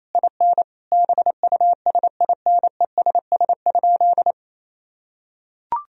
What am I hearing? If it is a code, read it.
Morse code, heard as a keyed tone, about 28 wpm, 700 Hz. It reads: IN BUSINESS?
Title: Sentences from Top 1000 Words - Review Part 3 - 28wpm